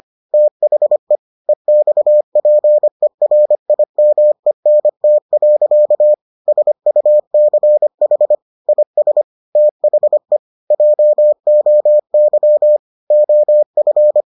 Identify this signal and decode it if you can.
Morse code, heard as a keyed tone, about 25 wpm, 600 Hz. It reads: THE EXPERIMENT. SUCH IS THE JOY OF